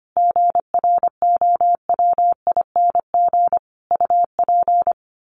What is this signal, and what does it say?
Morse code, heard as a keyed tone, about 25 wpm, 700 Hz. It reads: GROWING UP
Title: Sets of 5 Words Encore 25wpm